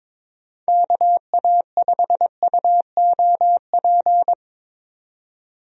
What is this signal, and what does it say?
Morse code, heard as a keyed tone, about 22 wpm, 700 Hz. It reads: KA5UOP
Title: USA Call Signs - 6 Characters - 22wpm